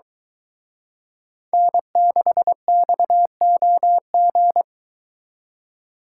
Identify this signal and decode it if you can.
Morse code, heard as a keyed tone, about 23 wpm, 700 Hz. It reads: N6XOG